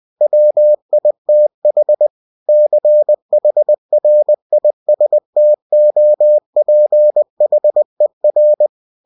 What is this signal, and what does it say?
Morse code, heard as a keyed tone, about 20 wpm, 600 Hz. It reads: WITH CHRISTOPHER
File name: winnie-20wpm